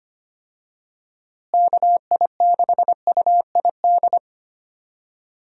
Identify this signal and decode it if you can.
Morse code, heard as a keyed tone, about 25 wpm, 700 Hz. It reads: KI6UID
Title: USA Call Signs - 6 Characters - Repeated 2x - 25wpm